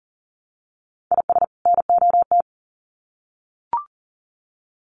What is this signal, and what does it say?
Morse code, heard as a keyed tone, about 40 wpm, 700 Hz. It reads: IS NOT
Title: Sets of 2 Words 40wpm